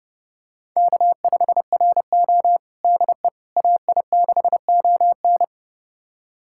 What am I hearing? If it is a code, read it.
Morse code, heard as a keyed tone, about 30 wpm, 700 Hz. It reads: K5RO DE AI6ON
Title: USA Callsign DE Callsign 30wpm